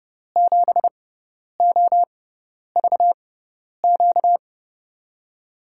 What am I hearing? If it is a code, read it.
Morse code, heard as a keyed tone, about 30 wpm, spaced as for 10 wpm, 700 Hz. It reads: 7OVQ
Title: USA Call Signs - 6 Characters - 6x Character Spacing - 30wpm